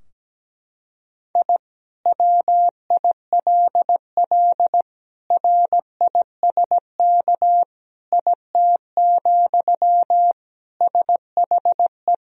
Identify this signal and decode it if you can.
Morse code, heard as a keyed tone, about 17 wpm, 700 Hz. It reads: I WILL RISK IT, SHE
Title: Sentences from Top 1000 Words - Review Part 4 - 17wpm